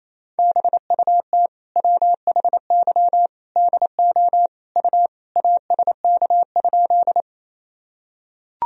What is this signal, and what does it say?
Morse code, heard as a keyed tone, about 27 wpm, 700 Hz. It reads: BUT WHY DO U ASK?
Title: Sentences from Top 200 Words and Abbreviations 28wpm